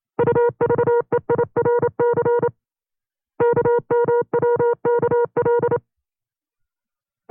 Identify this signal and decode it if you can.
Morse code, heard as a keyed tone, about 27 wpm, 480 Hz. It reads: UVEIRC KMWKL
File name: FreeFiles / CW-practice / 25WPM-side2